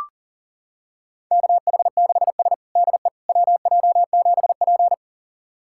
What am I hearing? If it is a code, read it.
Morse code, heard as a keyed tone, about 37 wpm, 700 Hz. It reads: KH6S DE WJ7P
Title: USA Callsign DE Callsign - 4 Characters - 40wpm